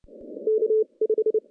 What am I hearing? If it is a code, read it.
Morse code, heard as a keyed tone, about 29 wpm, spaced as for 24 wpm, 440 Hz. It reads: K5
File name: contests / CW-SS-k5tr- -0600